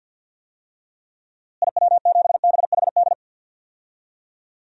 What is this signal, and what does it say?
Morse code, heard as a keyed tone, about 50 wpm, 700 Hz. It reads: IW7BHD
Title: DXCC-100 Call Signs - 6 Characters - Repeated 3x - 50wpm